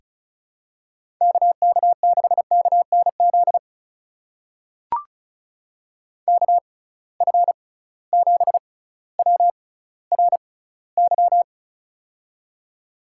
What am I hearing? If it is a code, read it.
Morse code, heard as a keyed tone, about 36 wpm, spaced as for 18 wpm, 700 Hz. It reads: KK6KNZ KF7WRY
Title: USA Call Signs - 6 Characters - 6x Character Spacing - 35wpm